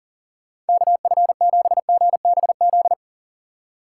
Morse code, heard as a keyed tone, about 39 wpm, 700 Hz. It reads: KF7GBZ
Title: USA Call Signs - 6 Characters - 3x Character Spacing - 40wpm